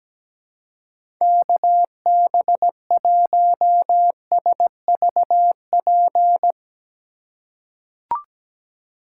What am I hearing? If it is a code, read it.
Morse code, heard as a keyed tone, about 17 wpm, 700 Hz. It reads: KB1SVP
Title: USA Call Signs - 6 Characters - 2x Character Spacing - 17wpm